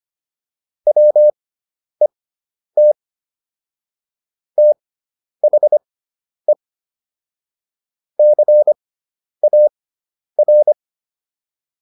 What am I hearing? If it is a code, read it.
Morse code, heard as a keyed tone, about 25 wpm, spaced as for 10 wpm, 600 Hz. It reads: WET THE CAR